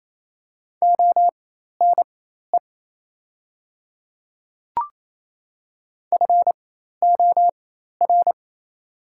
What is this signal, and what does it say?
Morse code, heard as a keyed tone, about 28 wpm, spaced as for 13 wpm, 700 Hz. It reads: ONE FOR
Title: 4x Character Spacing - Top 100 Words - 28wpm